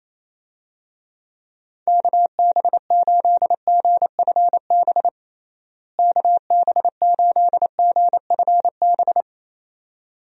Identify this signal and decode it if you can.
Morse code, heard as a keyed tone, about 28 wpm, 700 Hz. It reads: KB8GFB KB8GFB
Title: USA Call Signs - 6 Characters - Repeated 2x - 28wpm